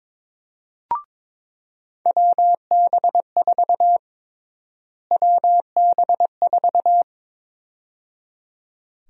Morse code, heard as a keyed tone, about 22 wpm, 700 Hz. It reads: WB4 WB4
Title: USA Call Sign Prefixes - Repeated 2x - 22wpm